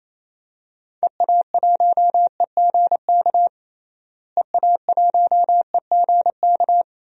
Morse code, heard as a keyed tone, about 28 wpm, 700 Hz. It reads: EA1EGK EA1EGK
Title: DXCC-100 Call Signs - 6 Characters - Repeated 2x - 28wpm